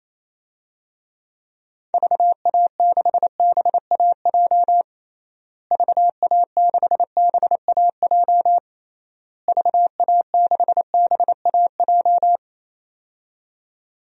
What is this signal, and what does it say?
Morse code, heard as a keyed tone, about 28 wpm, 700 Hz. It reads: VA6BAJ VA6BAJ VA6BAJ
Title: DXCC-100 Call Signs - 6 Characters - Repeated 3x - 28wpm